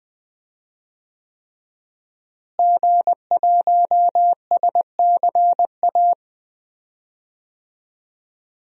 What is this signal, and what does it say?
Morse code, heard as a keyed tone, about 20 wpm, 700 Hz. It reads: G1SCA
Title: DXCC-100 Call Signs - 5 Characters - Repeated 3x - 20wpm